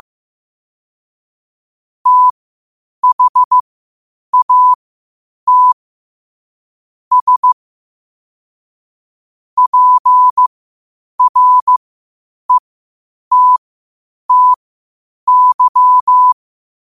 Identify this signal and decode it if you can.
Morse code, heard as a keyed tone, about 15 wpm, spaced as for 8 wpm, 1000 Hz. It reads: THAT S PRETTY
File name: qotdfiles / 8wpm